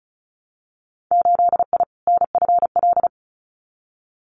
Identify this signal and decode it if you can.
Morse code, heard as a keyed tone, about 36 wpm, 700 Hz. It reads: 8I NFL
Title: Winter Field Day Exchanges Encore 35wpm